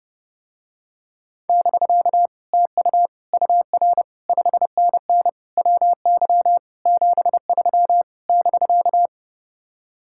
Morse code, heard as a keyed tone, about 30 wpm, 700 Hz. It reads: <BK> TU UR 5NN WY 73 <BK>